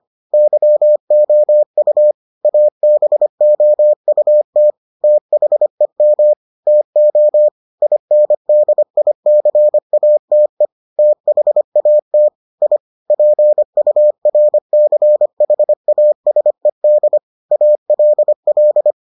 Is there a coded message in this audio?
Yes: Morse code from 0.3 to 18.9 s, heard as a keyed tone, about 25 wpm, 600 Hz. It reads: YOU ABOUT THEM TO INDICATE THAT I PURCHASED ALL